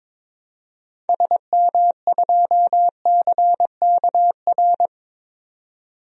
Morse code, heard as a keyed tone, about 22 wpm, 700 Hz. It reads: SM2CKR
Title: DXCC-100 Call Signs - 6 Characters - 22wpm